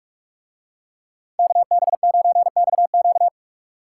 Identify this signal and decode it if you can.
Morse code, heard as a keyed tone, about 45 wpm, 700 Hz. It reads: KB9XQ